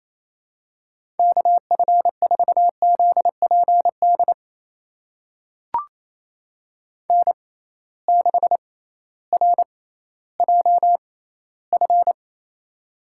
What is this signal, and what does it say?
Morse code, heard as a keyed tone, about 28 wpm, spaced as for 15 wpm, 700 Hz. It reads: KF4ZPD N6RJF
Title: USA Call Signs - 6x Character Spacing - 28wpm